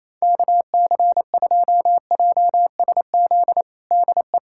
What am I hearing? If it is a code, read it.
Morse code, heard as a keyed tone, about 28 wpm, 700 Hz. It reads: KC2JSZ DE
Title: USA Callsign DE Callsign - 6 Characters - 28wpm